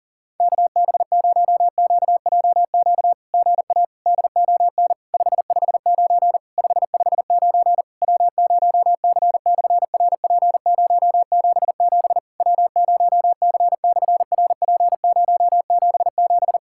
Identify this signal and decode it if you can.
Morse code, heard as a keyed tone, about 40 wpm, 700 Hz. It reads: KB0QJQ GA DON 559 559 W0C/RP077 W0C/RP077
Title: SOTA QSO Practice 40wpm